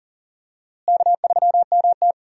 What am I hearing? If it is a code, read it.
Morse code, heard as a keyed tone, about 40 wpm, 700 Hz. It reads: K3MT
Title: USA Call Signs - 4 Characters - Repeated 2x - 40wpm